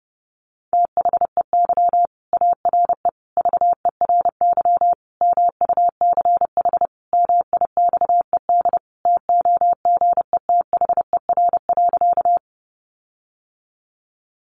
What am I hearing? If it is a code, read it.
Morse code, heard as a keyed tone, about 30 wpm, 700 Hz. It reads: THEY ARE VERY MUCH MIXED TOGETHER.
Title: Sentences from Top 1500 Words 30wpm